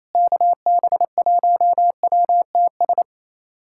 Morse code, heard as a keyed tone, about 27 wpm, 700 Hz. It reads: KB1WTS
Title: USA Callsign DE Callsign - 6 Characters - 28wpm